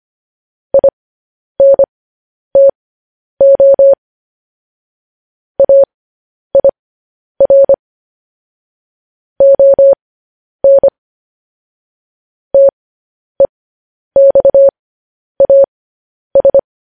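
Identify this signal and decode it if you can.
Morse code, heard as a keyed tone, about 25 wpm, spaced as for 10 wpm, 550 Hz. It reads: INTO AIR ON TEXAS
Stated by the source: Morse code news headlines sent at 10 wpm.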